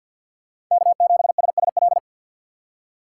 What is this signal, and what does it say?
Morse code, heard as a keyed tone, about 47 wpm, 700 Hz. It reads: K7SSL
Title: USA Call Signs - 4x Character Spacing - 50wpm